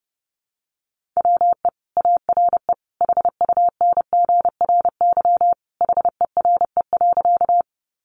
Morse code, heard as a keyed tone, about 30 wpm, 700 Hz. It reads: WE ARE HUNGRY HERE.
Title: Sentences from Top 1400 Words 30wpm